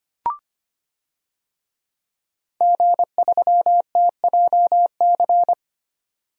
Morse code, heard as a keyed tone, about 25 wpm, 700 Hz. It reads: G3TJC
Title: DXCC-100 Call Signs - 5 Characters - 25wpm